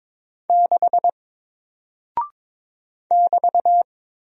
Morse code, heard as a keyed tone, about 22 wpm, 700 Hz. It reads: 6 <BT>
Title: Single Character (Letter:Number:Punctuation:Prosign) Alt 22wpm